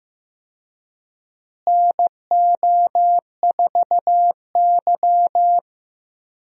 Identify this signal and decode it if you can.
Morse code, heard as a keyed tone, about 15 wpm, 700 Hz. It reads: NO4Y
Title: USA Call Signs - 4 Characters - Repeated 2x - 15wpm